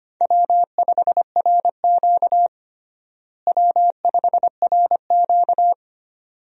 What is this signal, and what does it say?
Morse code, heard as a keyed tone, about 25 wpm, 700 Hz. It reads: W5RQ W5RQ
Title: USA Call Signs - 4 Characters - Repeated 2x - 25wpm